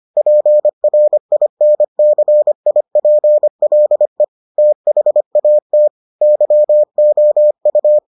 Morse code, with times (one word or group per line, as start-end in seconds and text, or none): 0.2-4.2 PRINCIPLE
4.6-5.9 THAT
6.2-8.0 YOU